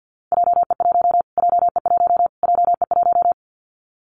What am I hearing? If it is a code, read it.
Morse code, heard as a keyed tone, about 49 wpm, 700 Hz. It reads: JE1 JE1 JE1
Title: Callsign Prefixes for Top 25 DXCC - Repeated 3x - 50wpm